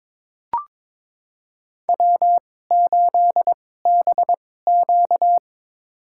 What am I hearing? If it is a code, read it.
Morse code, heard as a keyed tone, about 22 wpm, spaced as for 16 wpm, 700 Hz. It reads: W8BQ
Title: USA Call Signs - 4 Characters - 2x Character Spacing - 22wpm